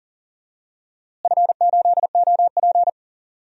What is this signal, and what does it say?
Morse code, heard as a keyed tone, about 40 wpm, 700 Hz. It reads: F8OP
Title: DXCC-100 Call Signs - 4 Characters - Repeated 3x - 40wpm